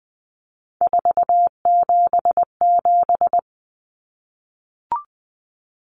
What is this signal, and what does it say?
Morse code, heard as a keyed tone, about 20 wpm, 700 Hz. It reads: 477
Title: Common Cut Numbers - 3-Digit 20wpm